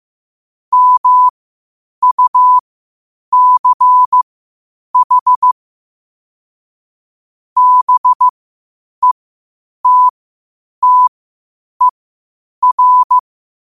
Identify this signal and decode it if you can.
Morse code, heard as a keyed tone, about 15 wpm, spaced as for 8 wpm, 1000 Hz. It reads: MUCH BETTER
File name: qotdfiles / 8wpm